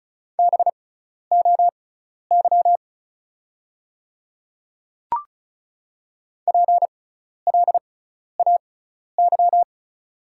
Morse code, heard as a keyed tone, about 36 wpm, spaced as for 12 wpm, 690 Hz. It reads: BOY PLAY
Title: 6x Character Spacing - Top 500 Words - 35wpm